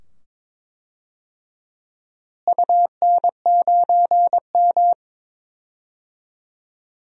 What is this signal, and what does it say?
Morse code, heard as a keyed tone, about 21 wpm, 700 Hz. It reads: UN9M
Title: DXCC-100 Call Signs - 4 Characters - Repeated 2x - 22wpm